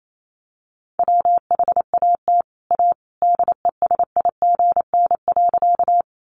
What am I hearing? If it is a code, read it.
Morse code, heard as a keyed tone, about 28 wpm, 700 Hz. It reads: WHAT A DESIGN.
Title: Sentences from Top 800 Words 28wpm